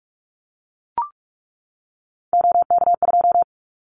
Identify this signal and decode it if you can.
Morse code, heard as a keyed tone, about 44 wpm, 700 Hz. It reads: OK2
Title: Callsign Prefixes for Top 25 DXCC - 45wpm